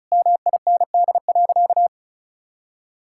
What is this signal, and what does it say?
Morse code, heard as a keyed tone, about 36 wpm, 700 Hz. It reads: MIND.
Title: Sentences from Top 1000 Words - Review Part 5 - 35wpm